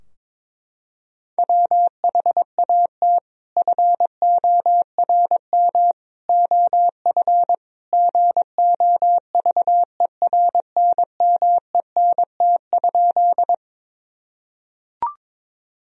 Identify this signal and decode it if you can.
Morse code, heard as a keyed tone, about 22 wpm, 700 Hz. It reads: WHAT FORM OF GOVERNMENT?
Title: Sentences from Top 1000 Words - Review Part 3 - 22wpm